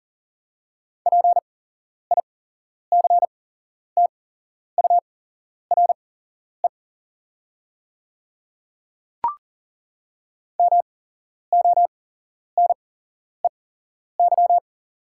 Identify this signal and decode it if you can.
Morse code, heard as a keyed tone, about 40 wpm, spaced as for 11 wpm, 700 Hz. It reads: PICTURE MONEY
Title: 8x Character Spacing - Top 500 Words - 40wpm